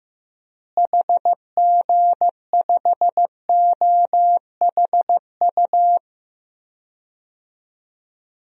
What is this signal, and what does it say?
Morse code, heard as a keyed tone, about 15 wpm, 700 Hz. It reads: HG5OHU
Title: DXCC-100 Call Signs - 6 Characters - Repeated 3x - 15wpm